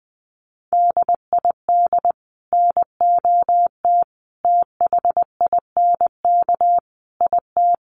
Morse code, heard as a keyed tone, about 20 wpm, 700 Hz. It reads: DID NOT THINK IT
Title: Sets of 5 Words Encore 20wpm